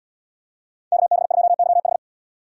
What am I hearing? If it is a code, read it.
Morse code, heard as a keyed tone, about 75 wpm, 700 Hz. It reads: KB2YH